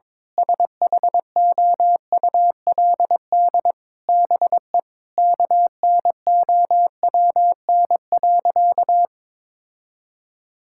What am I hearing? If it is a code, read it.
Morse code, heard as a keyed tone, about 22 wpm, 700 Hz. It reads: SHOULD BE KNOWN.
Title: Sentences from Top 1000 Words - Review Part 6 - 22wpm